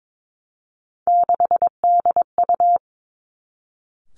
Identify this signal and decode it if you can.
Morse code, heard as a keyed tone, about 22 wpm, 700 Hz. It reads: 6DU
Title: Three Challenging Letter-Number Combinations 22wpm